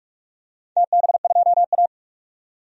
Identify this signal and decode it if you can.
Morse code, heard as a keyed tone, about 44 wpm, 700 Hz. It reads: TB2A
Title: DXCC-100 Call Signs - 4 Characters - 45wpm